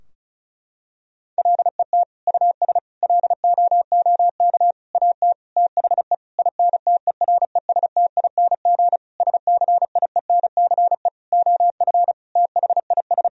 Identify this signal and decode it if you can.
Morse code, heard as a keyed tone, about 36 wpm, 700 Hz. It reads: LET US LOOK AT THE INTERESTING SCIENCE OF THIS
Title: Sentences from Top 1000 Words - Review Part 2 - 35wpm